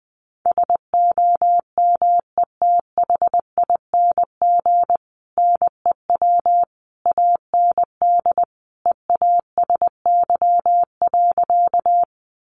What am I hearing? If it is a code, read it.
Morse code, heard as a keyed tone, about 20 wpm, 700 Hz. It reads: SOMETHING NEW AND EASY.
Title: Sentences from Top 700 Words 20wpm